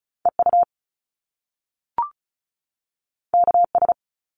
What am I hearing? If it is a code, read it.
Morse code, heard as a keyed tone, about 36 wpm, 700 Hz. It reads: EU KS
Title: Top Two Letter Combinations in Words 35wpm